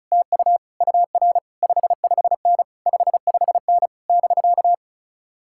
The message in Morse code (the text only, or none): TU UR 55N 55N <BK>